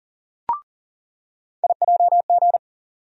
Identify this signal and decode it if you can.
Morse code, heard as a keyed tone, about 40 wpm, 700 Hz. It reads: IJG
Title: USA Call Sign Postfixes - 3 Letters 40wpm